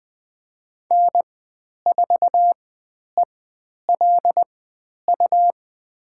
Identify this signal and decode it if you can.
Morse code, heard as a keyed tone, about 20 wpm, spaced as for 10 wpm, 700 Hz. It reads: N4ELU
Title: 109 Lesson 18g - Letter m - Call Signs 20wpm